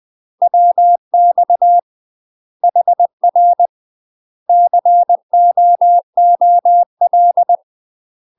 Morse code, heard as a keyed tone, about 20 wpm, 700 Hz. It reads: WX HR COOL
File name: UK MASTER QSOs TRAINING / 20WPM / UK MASTER QSOs 0062